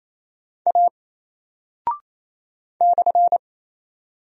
Morse code, heard as a keyed tone, about 28 wpm, 700 Hz. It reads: A /
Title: Single Character (Letter:Number:Punctuation:Prosign) Alt 28wpm